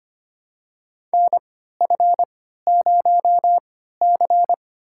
Morse code, heard as a keyed tone, about 25 wpm, spaced as for 14 wpm, 700 Hz. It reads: NF0C
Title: USA Call Signs - 4 Characters - 3x Character Spacing - 25wpm